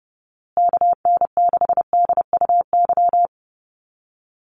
Morse code, heard as a keyed tone, about 30 wpm, 700 Hz. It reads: KN6DUY
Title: USA Call Signs Repeated 3x 30wpm